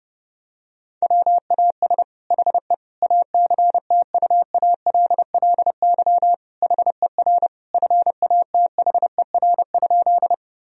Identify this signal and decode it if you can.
Morse code, heard as a keyed tone, about 30 wpm, 700 Hz. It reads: WAS HE ACTUALLY HER FATHER?
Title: Sentences from Top 1000 Words - Review Part 4 - 30wpm